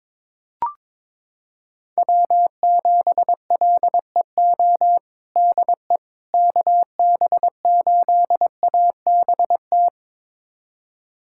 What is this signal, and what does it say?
Morse code, heard as a keyed tone, about 22 wpm, 700 Hz. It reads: W7LEO DE KB8ABT
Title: USA Callsign DE Callsign 22wpm